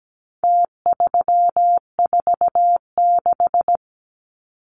Morse code, heard as a keyed tone, about 17 wpm, 700 Hz. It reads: T346
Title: Numbers - Contest Style - 4-Digit 17wpm